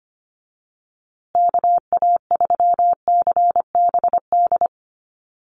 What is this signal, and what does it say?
Morse code, heard as a keyed tone, about 25 wpm, 700 Hz. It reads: KA3CBD
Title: USA Call Signs Repeated 3x 25wpm